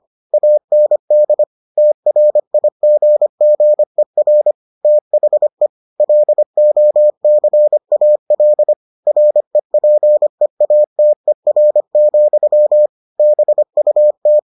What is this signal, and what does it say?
Morse code, heard as a keyed tone, about 25 wpm, 600 Hz. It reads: AND TRIGGER THE LOCAL REPEATER, BUT